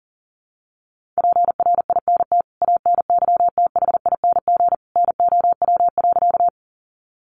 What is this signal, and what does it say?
Morse code, heard as a keyed tone, about 40 wpm, 700 Hz. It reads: PRINT ANYTHING NOW.
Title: Sentences from Top 1400 Words 40wpm